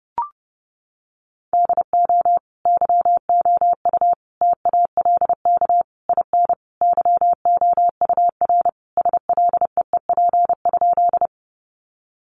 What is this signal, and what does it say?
Morse code, heard as a keyed tone, about 30 wpm, 700 Hz. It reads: DO YOU TALK IN YOUR SLEEP?
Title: Sentences from Top 500 Words 30wpm